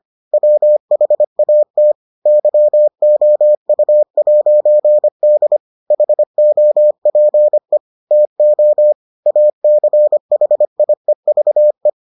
Morse code, heard as a keyed tone, about 25 wpm, 600 Hz. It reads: WHAT YOU'D HOPE TO ACHIEVE